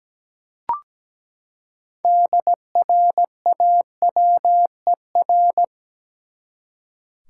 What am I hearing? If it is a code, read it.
Morse code, heard as a keyed tone, about 17 wpm, 700 Hz. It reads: DRAWER
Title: Six Letter Words - 17wpm